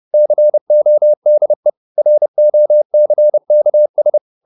Morse code, heard as a keyed tone, about 30 wpm, 600 Hz. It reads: CODE ROCKS